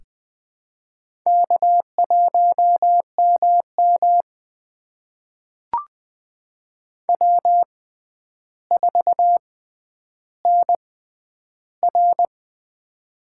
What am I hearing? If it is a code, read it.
Morse code, heard as a keyed tone, about 20 wpm, spaced as for 10 wpm, 700 Hz. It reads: K1MM W4NR
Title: USA Call Signs - 4 Characters - 6x Character Spacing - 20wpm